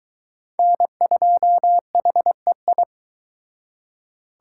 Morse code, heard as a keyed tone, about 22 wpm, 700 Hz. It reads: N2HEI